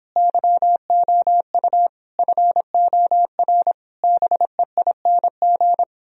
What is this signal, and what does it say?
Morse code, heard as a keyed tone, about 31 wpm, 700 Hz. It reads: YOU FOR BEING